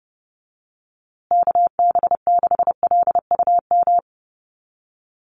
Morse code, heard as a keyed tone, about 30 wpm, 700 Hz. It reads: KB6LUM
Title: USA Call Signs Repeated 2x 30wpm